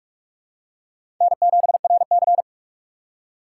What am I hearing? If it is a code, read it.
Morse code, heard as a keyed tone, about 46 wpm, 700 Hz. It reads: N7RC
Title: USA Call Signs - 4 Characters - 2x Character Spacing - 45wpm